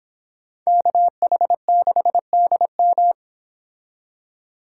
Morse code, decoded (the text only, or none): KH6DM